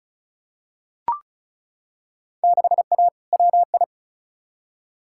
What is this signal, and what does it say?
Morse code, heard as a keyed tone, about 34 wpm, 700 Hz. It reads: 6A WI
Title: Field Day Exchanges - 35wpm